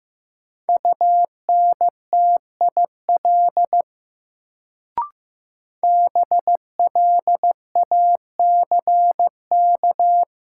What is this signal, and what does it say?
Morse code, heard as a keyed tone, about 15 wpm, 700 Hz. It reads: UNTIL BLACK
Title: Sets of 5 Words Encore - New Word Drill - 15wpm